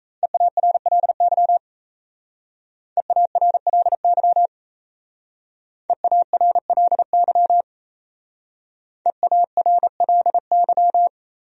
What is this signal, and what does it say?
Morse code, heard as a keyed tone, about 35 wpm, 700 Hz. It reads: EARLY EARLY EARLY EARLY
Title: Speed-Racing Top 500 Words 28wpm